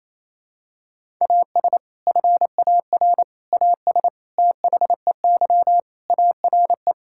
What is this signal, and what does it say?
Morse code, heard as a keyed tone, about 28 wpm, 700 Hz. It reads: AS FAR AS THEY ARE
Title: Sentences from Top 1000 Words - Review Part 6 - 28wpm